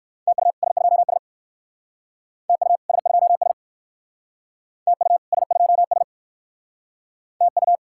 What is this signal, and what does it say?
Morse code, heard as a keyed tone, about 59 wpm, 700 Hz. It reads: TU S2S TU S2S TU S2S TU